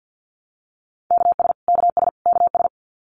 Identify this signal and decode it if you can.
Morse code, heard as a keyed tone, about 50 wpm, 700 Hz. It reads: KS KS KS
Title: US States Repeated 3x 50wpm